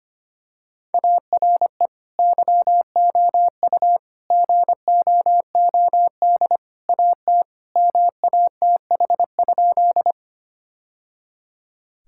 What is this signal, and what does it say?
Morse code, heard as a keyed tone, about 25 wpm, 700 Hz. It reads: ARE YOU GOOD AT MATH?
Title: Sentences from Top 1000 Words - Review Part 6 - 25wpm